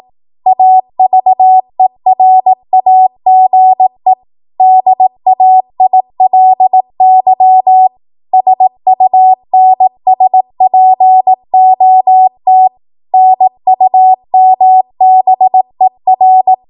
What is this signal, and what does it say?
Morse code, heard as a keyed tone, about 18 wpm, 750 Hz. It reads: AVERAGE DAILY SUNSPOT NUMBER
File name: Archive / 18 WPM / 18WPM